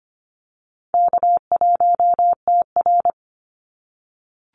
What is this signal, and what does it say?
Morse code, heard as a keyed tone, about 25 wpm, 700 Hz. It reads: K1TR
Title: Easy Call Signs - 25wpm